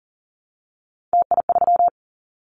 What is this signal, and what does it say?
Morse code, heard as a keyed tone, about 38 wpm, 700 Hz. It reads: TI3
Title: Callsign Prefixes for Top 100 DXCC - 40wpm